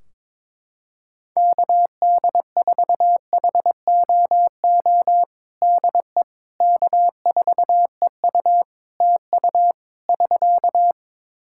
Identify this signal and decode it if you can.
Morse code, heard as a keyed tone, about 23 wpm, 700 Hz. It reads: KD4HOO DE K4EU TU <SK>
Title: Practice QSO - Exchange 4 - Ending - 22wpm